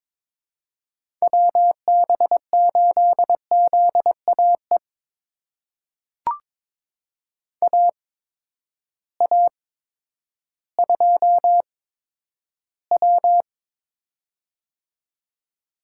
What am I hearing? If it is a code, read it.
Morse code, heard as a keyed tone, about 22 wpm, spaced as for 11 wpm, 700 Hz. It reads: WB8ZAE AA2W
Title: USA Call Signs - 8x Character Spacing - 22wpm